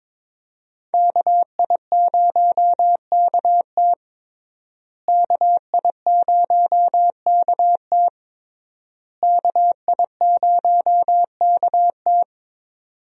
Morse code, heard as a keyed tone, about 21 wpm, 700 Hz. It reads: KI0KT KI0KT KI0KT
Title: USA Call Signs - 5 Characters - Repeated 3x - 22wpm